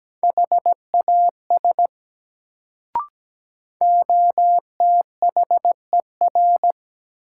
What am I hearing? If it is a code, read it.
Morse code, heard as a keyed tone, about 17 wpm, 700 Hz. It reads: HAS OTHER
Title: Sets of 2 Words - New Word Drill - 17wpm